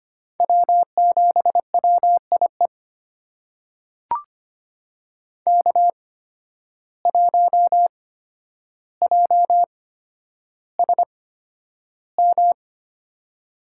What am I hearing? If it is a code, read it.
Morse code, heard as a keyed tone, about 25 wpm, spaced as for 11 wpm, 700 Hz. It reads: W7WIE K1JSM
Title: USA Call Signs - 5 Characters - 8x Character Spacing - 25wpm